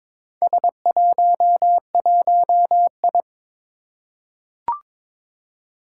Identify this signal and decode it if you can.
Morse code, heard as a keyed tone, about 21 wpm, 700 Hz. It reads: S11I
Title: Four Challenging Letter-Number Combinations 22wpm